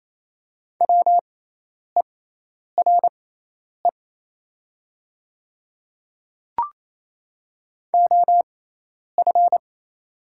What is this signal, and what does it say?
Morse code, heard as a keyed tone, about 28 wpm, spaced as for 10 wpm, 700 Hz. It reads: WERE OF
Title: 6x Character Spacing - Top 100 Words - 28wpm